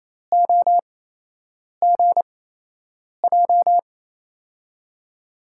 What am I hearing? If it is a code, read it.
Morse code, heard as a keyed tone, about 28 wpm, spaced as for 8 wpm, 700 Hz. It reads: OGJ
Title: USA Call Signs - 5 Characters - 8x Character Spacing - 28wpm